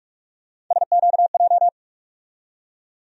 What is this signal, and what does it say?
Morse code, heard as a keyed tone, about 47 wpm, 700 Hz. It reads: SQJ
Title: USA Call Sign Postfixes - 3 Letters 45wpm